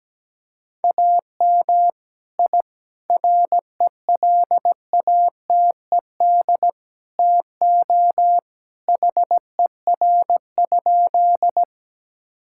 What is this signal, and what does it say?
Morse code, heard as a keyed tone, about 17 wpm, 700 Hz. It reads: AM I RELATED TO HER?
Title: Sentences from Top 1000 Words - Review Part 5 - 17wpm